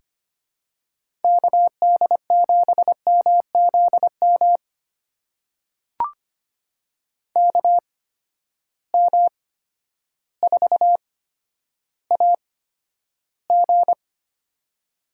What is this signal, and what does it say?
Morse code, heard as a keyed tone, about 25 wpm, spaced as for 11 wpm, 700 Hz. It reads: KD7MZM KM4AG
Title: USA Call Signs - 8x Character Spacing - 25wpm